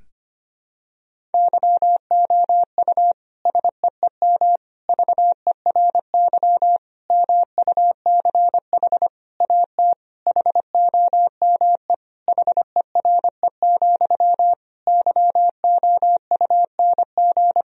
Morse code, heard as a keyed tone, about 25 wpm, 700 Hz. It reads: YOU SEEM VERY MUCH AT HOME HERE, YOUNG
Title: Sentences from Top 1000 Words - Review Part 1 - 25wpm